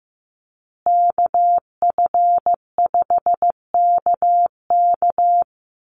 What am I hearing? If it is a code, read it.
Morse code, heard as a keyed tone, about 15 wpm, 700 Hz. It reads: KF5KK
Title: USA Call Signs Repeated 3x 15wpm